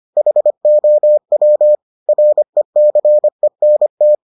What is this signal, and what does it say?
Morse code, heard as a keyed tone, about 25 wpm, 600 Hz. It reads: HOW RECENT